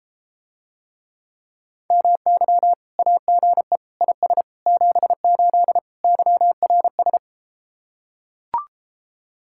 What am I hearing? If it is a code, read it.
Morse code, heard as a keyed tone, about 34 wpm, 700 Hz. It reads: MY AGE IS 78 YRS